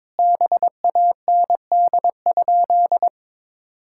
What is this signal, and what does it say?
Morse code, heard as a keyed tone, about 22 wpm, 700 Hz. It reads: BAND?
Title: Sentences from Top 1600 Words 22wpm